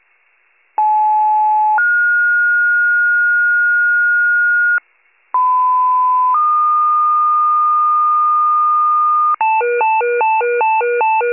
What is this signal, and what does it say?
Two-tone page, heard as a one-second tone, then a three-second tone, about 860 then 1430 Hz, about 1010 then 1220 Hz, with noise. It is heard from 0.8 to 9.4 s.
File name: pager / 94 RESCUE / wcalert